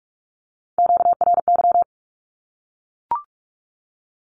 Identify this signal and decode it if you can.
Morse code, heard as a keyed tone, about 44 wpm, 700 Hz. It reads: QRY
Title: Q-Codes Repeated 3x 45wpm